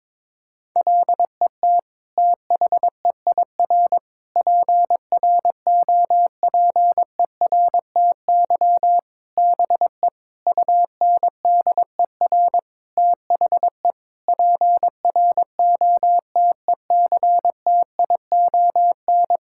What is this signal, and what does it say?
Morse code, heard as a keyed tone, about 22 wpm, 700 Hz. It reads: LET THEIR PROPERTY BE UNDER THE PROTECTION
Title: Sentences from Top 1600 Words 22wpm